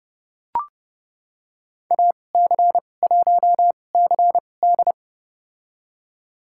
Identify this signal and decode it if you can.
Morse code, heard as a keyed tone, about 32 wpm, spaced as for 22 wpm, 700 Hz. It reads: AC1CD
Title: USA Call Signs - 5 Characters - 2x Character Spacing - 30wpm